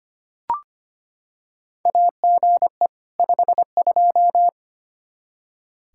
Morse code, heard as a keyed tone, about 25 wpm, 700 Hz. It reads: AGE 52
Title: Practice QSO - Exchange 3 - More Info - 25wpm